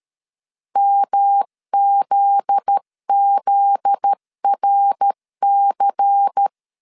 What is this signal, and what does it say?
Morse code, heard as a keyed tone, about 13 wpm, 800 Hz. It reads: MZZRC
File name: LEZIONI / 7 GF9Z / 7 GF9Z 12wpm 2147